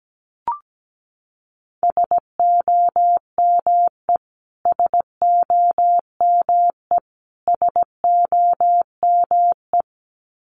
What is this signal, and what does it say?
Morse code, heard as a keyed tone, about 17 wpm, 700 Hz. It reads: SOME SOME SOME
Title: Top 200 Words Repeated 17wpm